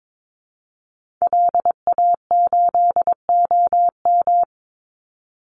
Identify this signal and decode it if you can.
Morse code, heard as a keyed tone, about 22 wpm, 700 Hz. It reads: LA8OM
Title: Call Signs - 22wpm